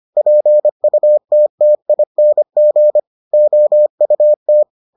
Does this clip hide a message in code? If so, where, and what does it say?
Yes: Morse code from 0.2 to 4.6 s, heard as a keyed tone, about 25 wpm, 600 Hz. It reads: PUTTING OUT